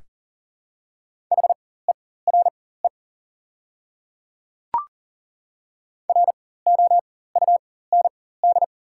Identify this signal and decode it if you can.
Morse code, heard as a keyed tone, about 40 wpm, spaced as for 19 wpm, 700 Hz. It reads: HERE ROUND
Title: 4x Character Spacing - Top 300 Words - 40wpm